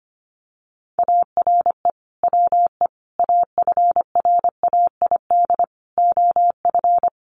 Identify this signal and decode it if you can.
Morse code, heard as a keyed tone, about 25 wpm, 700 Hz. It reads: ARE WE AFRAID OF
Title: Sentences from Top 1500 Words 25wpm